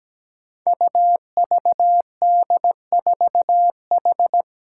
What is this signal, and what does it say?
Morse code, heard as a keyed tone, about 17 wpm, 700 Hz. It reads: UVD4H
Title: Five Challenging Letter-Number Combinations 17wpm